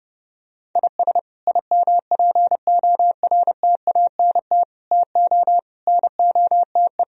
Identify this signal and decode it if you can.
Morse code, heard as a keyed tone, about 29 wpm, 700 Hz. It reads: IS IMPORTANT TO NOTE